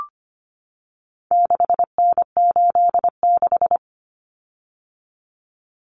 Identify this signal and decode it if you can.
Morse code, heard as a keyed tone, about 25 wpm, 700 Hz. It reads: 6N86
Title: Common Cut Numbers - 4-Digit 25wpm